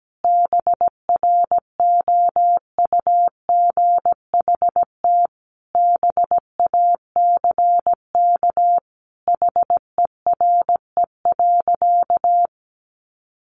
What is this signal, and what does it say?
Morse code, heard as a keyed tone, about 17 wpm, 700 Hz. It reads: BROUGHT BACK HERE.
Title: Sentences from Top 600 Words 17wpm